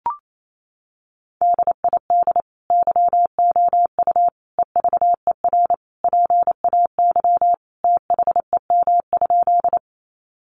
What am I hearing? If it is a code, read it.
Morse code, heard as a keyed tone, about 28 wpm, 700 Hz. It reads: DID YOU EVER PAY THEM?
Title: Sentences from Top 700 Words 28wpm